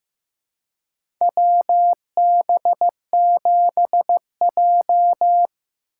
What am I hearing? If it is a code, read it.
Morse code, heard as a keyed tone, about 15 wpm, 700 Hz. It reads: WB7J
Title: USA Call Signs - 4 Characters - 4x Character Spacing - 15wpm